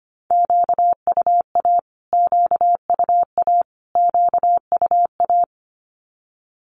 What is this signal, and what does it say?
Morse code, heard as a keyed tone, about 25 wpm, 700 Hz. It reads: QUA QUA QUA
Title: Q-Codes Repeated 3x 25wpm